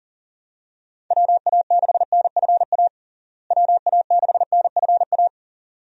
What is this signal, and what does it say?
Morse code, heard as a keyed tone, about 40 wpm, 700 Hz. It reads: WA6NFA WA6NFA
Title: USA Call Signs - 6 Characters - Repeated 2x - 40wpm